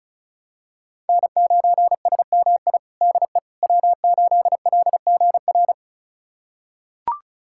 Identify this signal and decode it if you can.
Morse code, heard as a keyed tone, about 36 wpm, 700 Hz. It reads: N9SMI DE W8LGR
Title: USA Callsign DE Callsign - 5 Characters - 35wpm